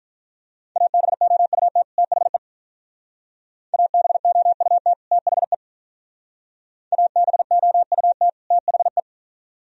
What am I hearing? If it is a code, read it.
Morse code, heard as a keyed tone, about 43 wpm, 700 Hz. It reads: ABOUT THE ABOUT THE ABOUT THE
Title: Speed-Racing - Sets of 2 Words 35wpm